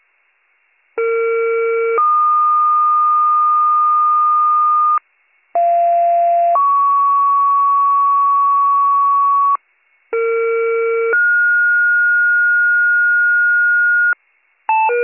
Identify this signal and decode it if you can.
Two-tone page, heard as a one-second tone, then a three-second tone, about 460 then 1220 Hz, about 690 then 1120 Hz, about 460 then 1430 Hz. It is heard from 1.0 to 14.1 s.